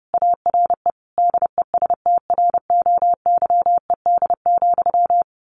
Morse code, heard as a keyed tone, about 30 wpm, 700 Hz. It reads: ARE DESTROYED,
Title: A Sound Of Thunder 30wpm